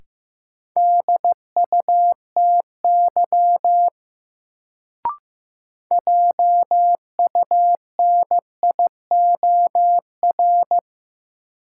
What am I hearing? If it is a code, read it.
Morse code, heard as a keyed tone, about 15 wpm, 700 Hz. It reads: DUTY JUNIOR
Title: Sentences from Top 1500 Words - New Word Drill Alt - 15wpm